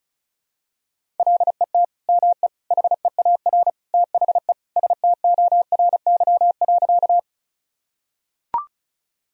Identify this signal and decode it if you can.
Morse code, heard as a keyed tone, about 34 wpm, 700 Hz. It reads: LET ME HEAR THE STORY.
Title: Sentences from Top 1000 Words - Review Part 1 - 35wpm